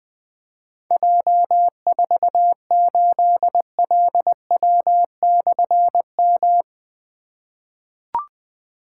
Morse code, heard as a keyed tone, about 20 wpm, 700 Hz. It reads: J48LW/M
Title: Difficult Call Signs 20wpm